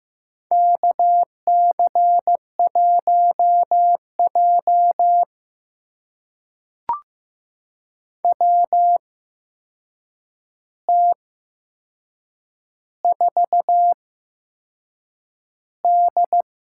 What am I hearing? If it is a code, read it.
Morse code, heard as a keyed tone, about 15 wpm, spaced as for 6 wpm, 700 Hz. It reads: KC1J WT4D
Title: USA Call Signs - 4 Characters - 8x Character Spacing - 15wpm